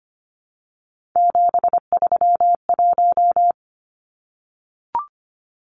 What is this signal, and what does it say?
Morse code, heard as a keyed tone, about 25 wpm, 700 Hz. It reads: 731
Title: Numbers - 3-Digit 25wpm